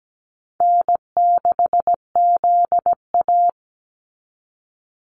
Morse code, heard as a keyed tone, about 17 wpm, 700 Hz. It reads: N6ZA
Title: NA SOTA Activator Callsigns 17wpm